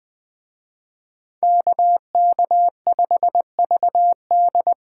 Morse code, heard as a keyed tone, about 25 wpm, 700 Hz. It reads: KK5VD
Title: Speed Racing - USA 2x2 Call Signs 20wpm